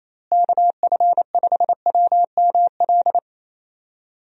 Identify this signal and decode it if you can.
Morse code, heard as a keyed tone, about 27 wpm, 700 Hz. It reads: KF5WML
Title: USA Call Signs - 6 Characters - Repeated 3x - 28wpm